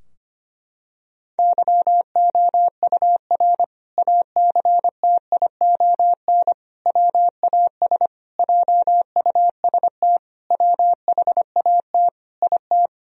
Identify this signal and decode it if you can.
Morse code, heard as a keyed tone, about 25 wpm, 700 Hz. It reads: YOUR ACTION WAS JUST WHAT IT
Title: Sentences from Top 1000 Words - Review Part 4 - 25wpm